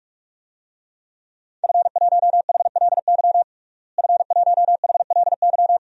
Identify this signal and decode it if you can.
Morse code, heard as a keyed tone, about 46 wpm, 700 Hz. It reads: F1HLY F1HLY
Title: DXCC-100 Call Signs - 5 Characters - Repeated 3x - 45wpm